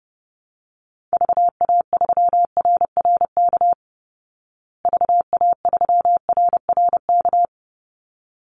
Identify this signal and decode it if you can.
Morse code, heard as a keyed tone, about 30 wpm, 700 Hz. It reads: VA3RRK VA3RRK
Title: Call Signs Repeated 3x - 30wpm